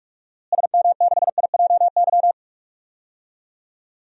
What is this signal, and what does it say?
Morse code, heard as a keyed tone, about 44 wpm, 700 Hz. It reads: SM6IJY
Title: DXCC-100 Call Signs - 6 Characters - Repeated 2x - 45wpm